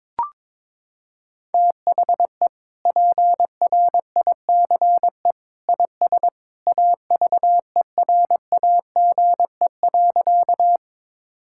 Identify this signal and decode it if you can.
Morse code, heard as a keyed tone, about 22 wpm, 700 Hz. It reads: THE PRICE IS AVERAGE.
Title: Sentences from Top 1000 Words - Review Part 6 - 22wpm